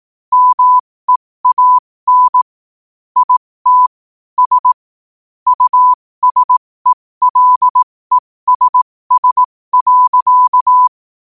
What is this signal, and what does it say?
Morse code, heard as a keyed tone, about 18 wpm, spaced as for 15 wpm, 1000 Hz. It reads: MEAN IT S USELESS.
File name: qotdfiles / 15wpm